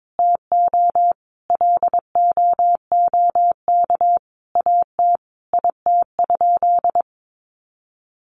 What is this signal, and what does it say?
Morse code, heard as a keyed tone, about 22 wpm, 700 Hz. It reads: TO LOOK AT IT?
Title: Sentences from Top 100 Words 22wpm